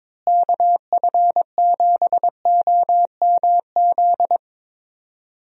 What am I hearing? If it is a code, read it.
Morse code, heard as a keyed tone, about 22 wpm, 700 Hz. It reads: KF7OMZ
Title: USA Call Signs - 6 Characters - Repeated 2x - 22wpm